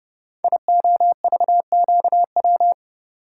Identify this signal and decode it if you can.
Morse code, heard as a keyed tone, about 30 wpm, 700 Hz. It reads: IOVQW
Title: Code Groups - 5 Letter:Numbers - 30wpm